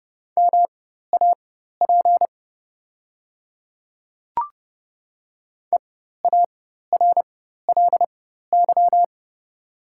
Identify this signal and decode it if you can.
Morse code, heard as a keyed tone, about 31 wpm, spaced as for 14 wpm, 700 Hz. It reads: MAP EARLY
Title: 4x Character Spacing - Top 500 Words - 30wpm